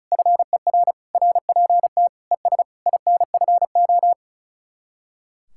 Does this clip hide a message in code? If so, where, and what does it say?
Yes: Morse code from 0.1 to 4.1 s, heard as a keyed tone, about 45 wpm, 700 Hz. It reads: FER RPT ES INFO